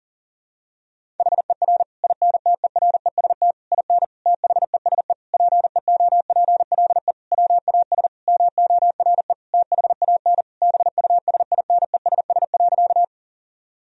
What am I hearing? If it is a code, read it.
Morse code, heard as a keyed tone, about 40 wpm, 700 Hz. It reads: HER INTEREST IN THESE PEOPLE WAS MORE THAN BUSINESS.
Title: Sentences from Top 1000 Words - Review Part 2 - 40wpm